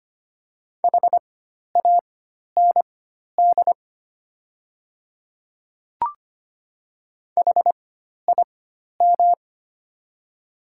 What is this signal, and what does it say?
Morse code, heard as a keyed tone, about 25 wpm, spaced as for 12 wpm, 690 Hz. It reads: HAND HIM
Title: 4x Character Spacing - Top 200 Words - 25wpm